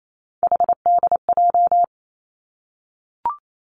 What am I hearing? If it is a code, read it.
Morse code, heard as a keyed tone, about 28 wpm, 700 Hz. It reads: HDJ
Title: Three Challenging Letter-Number Combinations 28wpm